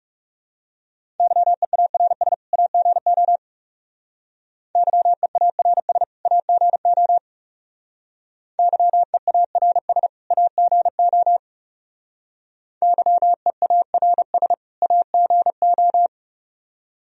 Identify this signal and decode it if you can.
Morse code, heard as a keyed tone, about 36 wpm, 700 Hz. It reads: YEARS AGO YEARS AGO YEARS AGO YEARS AGO
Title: Speed-Racing - Sets of 2 Words 30wpm